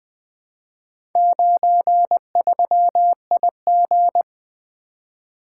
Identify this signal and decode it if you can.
Morse code, heard as a keyed tone, about 20 wpm, 700 Hz. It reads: 93IG
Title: Four Letter-Number Combinations - 20wpm